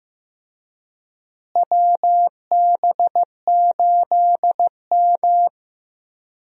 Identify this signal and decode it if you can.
Morse code, heard as a keyed tone, about 15 wpm, 700 Hz. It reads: WB8M
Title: USA Call Signs - 4 Characters - Repeated 3x - 15wpm